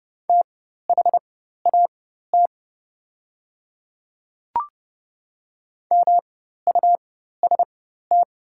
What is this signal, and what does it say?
Morse code, heard as a keyed tone, about 30 wpm, spaced as for 14 wpm, 700 Hz. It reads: THAT MUST
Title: 4x Character Spacing - Top 200 Words - 30wpm